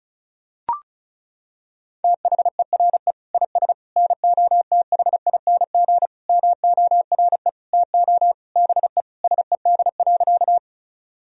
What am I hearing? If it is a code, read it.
Morse code, heard as a keyed tone, about 36 wpm, 700 Hz. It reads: THERE IS NOTHING MORE TO BE SED.
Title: Sentences from Top 500 Words and Abbreviations 35wpm